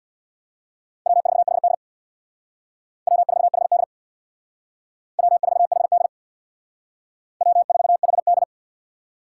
Morse code, heard as a keyed tone, about 60 wpm, 700 Hz. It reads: W4HD W4HD W4HD W4HD
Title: Speed Racing - USA 1x2 Call Signs 50wpm